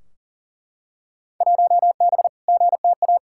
Morse code, heard as a keyed tone, about 40 wpm, 700 Hz. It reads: 1B GTA
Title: Field Day Exchanges Encore - 40wpm